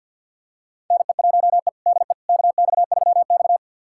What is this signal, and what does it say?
Morse code, heard as a keyed tone, about 47 wpm, 700 Hz. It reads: NE1E DE KX3X
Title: USA Callsign DE Callsign - 4 Characters - 50wpm